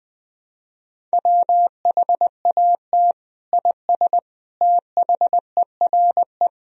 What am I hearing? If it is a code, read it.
Morse code, heard as a keyed tone, about 21 wpm, 700 Hz. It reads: WHAT IS THERE
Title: Sentences from Top 400 Words and Abbreviations 20wpm